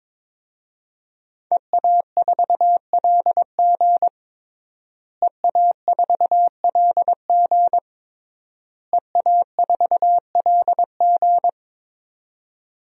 Morse code, heard as a keyed tone, about 22 wpm, 700 Hz. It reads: EA4LG EA4LG EA4LG
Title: DXCC-100 Call Signs - 5 Characters - Repeated 3x - 22wpm